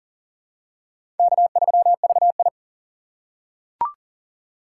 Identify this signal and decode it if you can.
Morse code, heard as a keyed tone, about 40 wpm, 700 Hz. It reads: K3VI
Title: USA Call Signs - 4 Characters - 4x Character Spacing - 40wpm